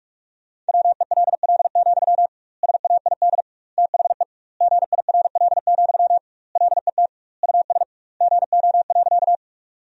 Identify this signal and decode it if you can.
Morse code, heard as a keyed tone, about 46 wpm, 700 Hz. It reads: WELL, SAID THE GIRL, LET US GO.
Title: 2x Word-Spacing - Sentences from Top 400 Words - 45wpm